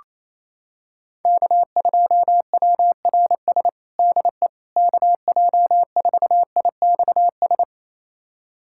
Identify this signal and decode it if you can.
Morse code, heard as a keyed tone, about 28 wpm, 700 Hz. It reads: K2WRS DE KJ4IXS
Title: USA Callsign DE Callsign 28wpm